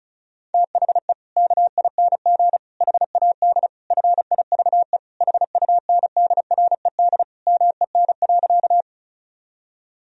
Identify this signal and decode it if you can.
Morse code, heard as a keyed tone, about 34 wpm, 700 Hz. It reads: THE KING HAD FIVE HUNDRED MEN.
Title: Sentences from Top 1000 Words - Review Part 2 - 35wpm